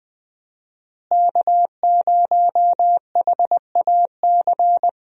Morse code, heard as a keyed tone, about 20 wpm, 700 Hz. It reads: K0HAC
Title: USA Call Signs - 5 Characters - Repeated 3x - 20wpm